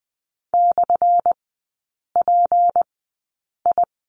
Morse code, heard as a keyed tone, about 20 wpm, spaced as for 8 wpm, 700 Hz. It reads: /PI
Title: 191 Lesson 31e - Letter v - Three Characters 20wpm